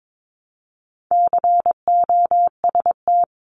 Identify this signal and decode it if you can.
Morse code, heard as a keyed tone, about 22 wpm, 700 Hz. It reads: COST
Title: Multi-letter Phonemes 22wpm